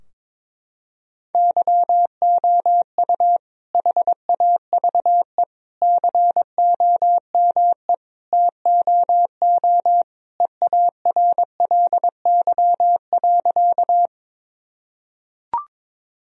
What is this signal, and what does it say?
Morse code, heard as a keyed tone, about 22 wpm, 700 Hz. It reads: YOU HAVE COME TOO EARLY.
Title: Sentences from Top 1000 Words - Review Part 2 - 22wpm